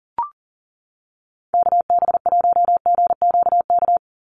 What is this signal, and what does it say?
Morse code, heard as a keyed tone, about 38 wpm, 690 Hz. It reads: KB1GQK
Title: USA Call Signs 40wpm